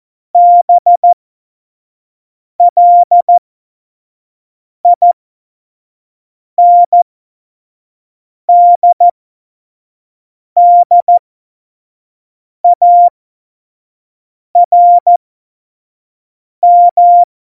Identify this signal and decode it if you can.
Morse code, heard as a keyed tone, about 14 wpm, spaced as for 5 wpm, 700 Hz. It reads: BLINDDARM
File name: Morsekurs / Witze / Witz 23. Januar 2018, 25 F70